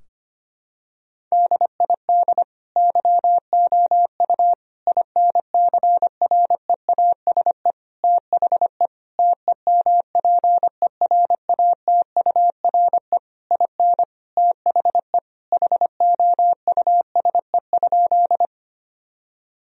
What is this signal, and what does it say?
Morse code, heard as a keyed tone, about 25 wpm, 700 Hz. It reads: DID YOU INCREASE THE TEMPERATURE IN THE HOUSE?
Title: Sentences from Top 1000 Words - Review Part 4 - 25wpm